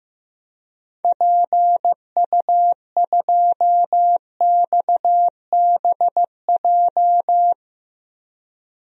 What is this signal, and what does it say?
Morse code, heard as a keyed tone, about 15 wpm, 700 Hz. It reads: PU2XBJ
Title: DXCC-100 Call Signs - 6 Characters - Repeated 2x - 15wpm